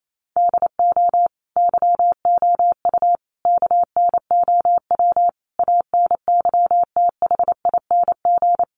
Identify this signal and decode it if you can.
Morse code, heard as a keyed tone, about 28 wpm, 700 Hz. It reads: DO YOU KNOW ANYTHING
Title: Sentences from Top 1000 Words 28wpm